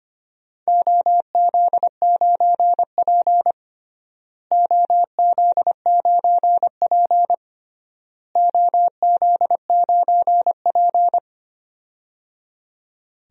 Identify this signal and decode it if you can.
Morse code, heard as a keyed tone, about 25 wpm, 700 Hz. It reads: OZ9P OZ9P OZ9P
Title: DXCC-100 Call Signs - 4 Characters - Repeated 3x - 25wpm